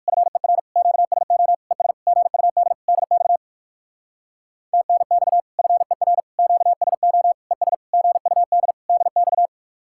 Morse code, heard as a keyed tone, about 59 wpm, 700 Hz. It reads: FER QSO ES GUD DX TNX FER QSO ES GUD DX